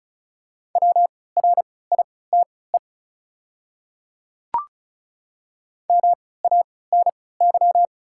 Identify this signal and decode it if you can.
Morse code, heard as a keyed tone, about 36 wpm, spaced as for 20 wpm, 690 Hz. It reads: WRITE MANY
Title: 3x Character Spacing - Top 100 Words - 35wpm